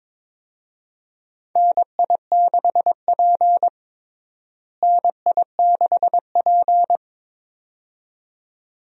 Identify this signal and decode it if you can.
Morse code, heard as a keyed tone, about 22 wpm, 700 Hz. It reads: NI6P NI6P
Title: USA Call Signs - 4 Characters - Repeated 2x - 22wpm